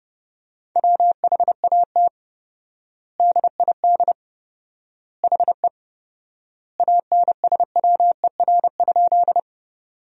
Morse code, heard as a keyed tone, about 30 wpm, 700 Hz. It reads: WHAT DID HE ANSWER?
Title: 4x Word-Spacing - Sentences from Top 300 Words - 30wpm